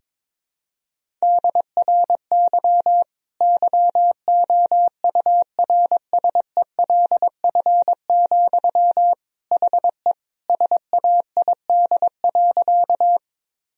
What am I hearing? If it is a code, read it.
Morse code, heard as a keyed tone, about 22 wpm, 700 Hz. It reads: DRY YOURSELF, HE SAID.
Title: Sentences from Top 1000 Words - Review Part 3 - 22wpm